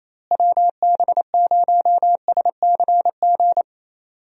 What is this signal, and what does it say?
Morse code, heard as a keyed tone, about 28 wpm, 700 Hz. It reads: WB0SCG
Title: USA Call Signs - 6 Characters - Repeated 2x - 28wpm